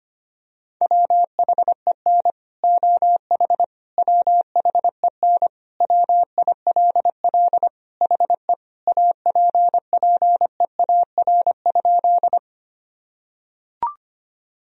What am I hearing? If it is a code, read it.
Morse code, heard as a keyed tone, about 25 wpm, 700 Hz. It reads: WHEN OH WHEN WILL HE APPEAR?
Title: Sentences from Top 1000 Words - Review Part 2 - 25wpm